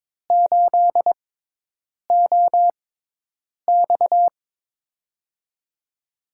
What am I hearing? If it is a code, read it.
Morse code, heard as a keyed tone, about 22 wpm, spaced as for 8 wpm, 700 Hz. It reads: 8OX
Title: USA Call Signs - 6x Character Spacing - 22wpm